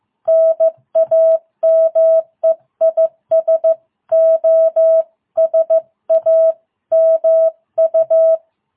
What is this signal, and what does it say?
Morse code, heard as a keyed tone, about 15 wpm, 660 Hz. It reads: NAME IS OSAMU